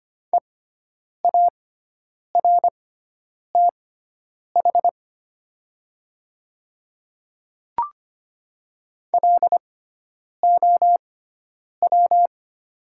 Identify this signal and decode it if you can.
Morse code, heard as a keyed tone, about 25 wpm, spaced as for 9 wpm, 700 Hz. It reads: EARTH LOW
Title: 6x Character Spacing - Top 200 Words - 25wpm